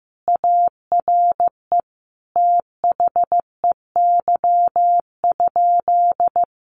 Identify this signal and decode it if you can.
Morse code, heard as a keyed tone, about 15 wpm, 700 Hz. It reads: ARE THEY?
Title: Sentences from Top 1200 Words 15wpm